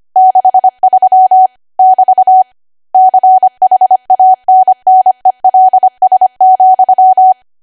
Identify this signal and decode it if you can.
Morse code, heard as a keyed tone, about 25 wpm, 750 Hz. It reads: 63 <BT> CHANNELS,
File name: Archive / 25 WPM / 25WPM